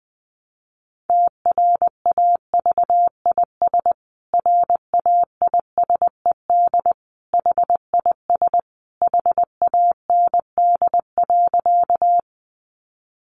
Morse code, heard as a keyed tone, about 20 wpm, 700 Hz. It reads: TRAVIS RAISED HIS HAND.
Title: A Sound Of Thunder 20wpm